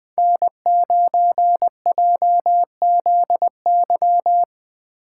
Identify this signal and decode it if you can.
Morse code, heard as a keyed tone, about 20 wpm, 700 Hz. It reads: N9JZY
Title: K1USN SST QSO Practice - 20wpm